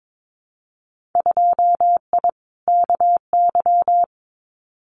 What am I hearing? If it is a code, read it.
Morse code, heard as a keyed tone, about 22 wpm, 700 Hz. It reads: 2I KY
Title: Winter Field Day Exchanges 22wpm